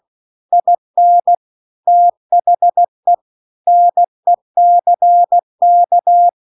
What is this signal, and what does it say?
Morse code, heard as a keyed tone, about 16 wpm, 700 Hz. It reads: IN THE NECK
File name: CW-The-Big-Twoheaded-River-2-16WPM